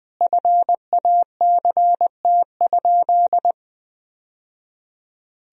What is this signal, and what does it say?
Morse code, heard as a keyed tone, about 20 wpm, 700 Hz. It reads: FACT?
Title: Sentences from Top 1000 Words - Review Part 4 - 20wpm